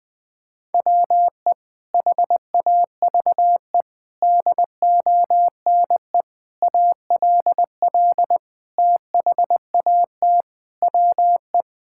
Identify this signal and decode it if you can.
Morse code, heard as a keyed tone, about 20 wpm, 700 Hz. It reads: WE HAVE DONE ALL THAT WE
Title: Sentences from Top 1000 Words - Review Part 5 - 20wpm